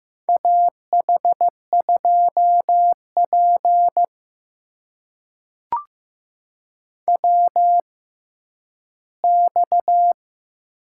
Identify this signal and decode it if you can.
Morse code, heard as a keyed tone, about 15 wpm, spaced as for 10 wpm, 700 Hz. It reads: AH2P WX
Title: USA Call Signs - 6x Character Spacing - 15wpm